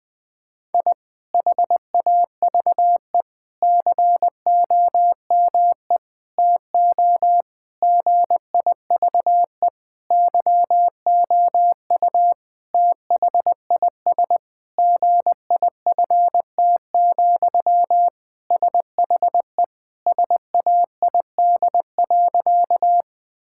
Morse code, heard as a keyed tone, about 20 wpm, 700 Hz. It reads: I HAVE COME TO GIVE YOU THIS GIFT, SHE SAID.
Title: Sentences from Top 1000 Words - Review Part 6 - 20wpm